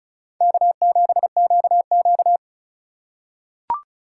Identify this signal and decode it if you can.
Morse code, heard as a keyed tone, about 36 wpm, 700 Hz. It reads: K7QQ
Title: USA Call Signs - 4 Characters - 3x Character Spacing - 35wpm